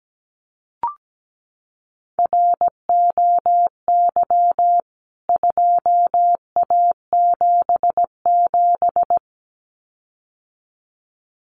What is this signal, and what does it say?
Morse code, heard as a keyed tone, about 17 wpm, 700 Hz. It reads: ROY 2A77
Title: CWT Exchange 17wpm